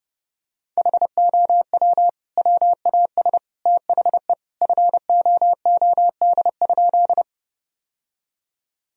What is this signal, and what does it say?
Morse code, heard as a keyed tone, about 30 wpm, 700 Hz. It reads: HOW WAS THE FOOD?
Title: Sentences from Top 1000 Words - Review Part 1 - 30wpm